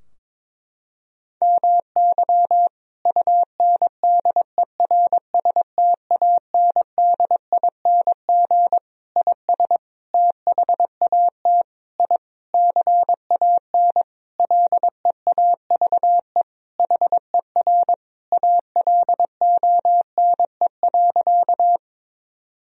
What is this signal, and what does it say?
Morse code, heard as a keyed tone, about 22 wpm, 700 Hz. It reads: MY UNDERSTANDING IS THAT I CAN LEAVE HER ALONE.
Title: Sentences from Top 1000 Words - Review Part 3 - 22wpm